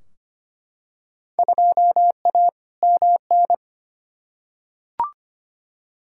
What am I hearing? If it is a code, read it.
Morse code, heard as a keyed tone, about 25 wpm, 700 Hz. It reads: 2A MN
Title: Field Day Exchanges - 25wpm